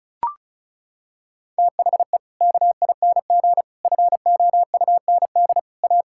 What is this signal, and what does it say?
Morse code, heard as a keyed tone, about 36 wpm, 700 Hz. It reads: THE KING FOUND A
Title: Sentences from Top 1000 Words - Review Part 5 - 35wpm